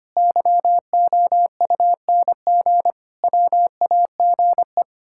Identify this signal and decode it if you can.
Morse code, heard as a keyed tone, about 25 wpm, 700 Hz. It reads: YOUNG WAGE
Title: Pangram Challenge v2 - 25wpm